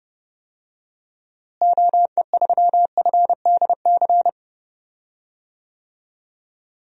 Morse code, heard as a keyed tone, about 29 wpm, 700 Hz. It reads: OE3FDC
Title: DXCC-100 Call Signs - 6 Characters - Repeated 2x - 30wpm